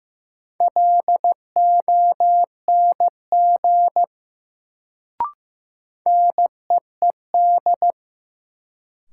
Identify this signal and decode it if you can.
Morse code, heard as a keyed tone, about 15 wpm, 700 Hz. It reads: LONG NEED
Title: Sets of 2 Words - New Word Drill - 15wpm